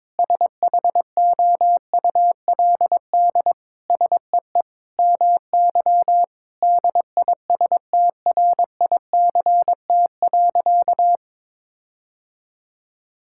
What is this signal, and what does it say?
Morse code, heard as a keyed tone, about 21 wpm, 700 Hz. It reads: SHOULD SEE MY DISTRICT.
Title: Sentences from Top 1600 Words 22wpm